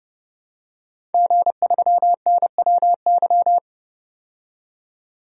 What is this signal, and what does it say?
Morse code, heard as a keyed tone, about 29 wpm, 700 Hz. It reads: G3NWY
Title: DXCC-100 Call Signs - 5 Characters - 30wpm